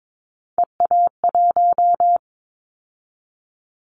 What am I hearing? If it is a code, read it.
Morse code, heard as a keyed tone, about 22 wpm, 700 Hz. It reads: EA1
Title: Callsign Prefixes for Top 25 DXCC - 22wpm